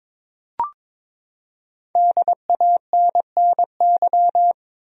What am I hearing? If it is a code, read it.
Morse code, heard as a keyed tone, about 22 wpm, 700 Hz. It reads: DANNY
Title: USA Common Names - Max 5 Letters - 22wpm